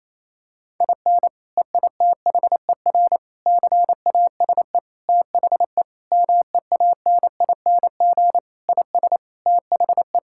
Morse code, heard as a keyed tone, about 28 wpm, 700 Hz. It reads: IN EITHER CASE THE MEANING IS THE
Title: Sentences from Top 1000 Words - Review Part 6 - 28wpm